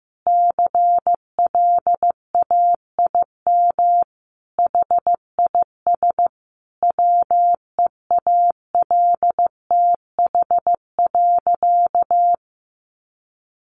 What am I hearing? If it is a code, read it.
Morse code, heard as a keyed tone, about 15 wpm, 700 Hz. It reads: CLAIM HIS WEALTH.
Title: Sentences from Top 1500 Words 15wpm